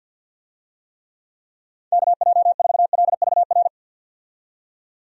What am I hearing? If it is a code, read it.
Morse code, heard as a keyed tone, about 53 wpm, 700 Hz. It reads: KJ4LVR
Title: USA Call Signs - 6 Characters - 50wpm